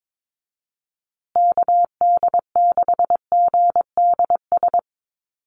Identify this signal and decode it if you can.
Morse code, heard as a keyed tone, about 22 wpm, 700 Hz. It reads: KD6GDS
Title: USA Call Signs Repeated 3x 22wpm